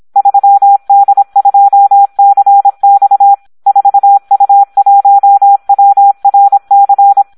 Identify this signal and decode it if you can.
Morse code, heard as a keyed tone, about 26 wpm, 800 Hz. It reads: 3D2CX 4U1WRC
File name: Audio practica CW / SD 1 26